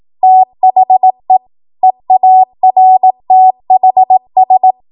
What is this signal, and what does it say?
Morse code, heard as a keyed tone, about 18 wpm, 750 Hz. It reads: THE EARTHS
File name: Archive / 18 WPM / 18WPM